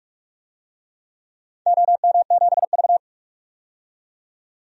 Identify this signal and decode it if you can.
Morse code, heard as a keyed tone, about 45 wpm, 700 Hz. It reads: OM7V
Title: DXCC-100 Call Signs - 4 Characters - 45wpm